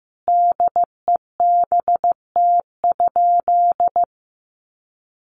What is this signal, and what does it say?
Morse code, heard as a keyed tone, about 15 wpm, 700 Hz. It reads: DEBT?
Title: Sentences from Top 1300 Words 15wpm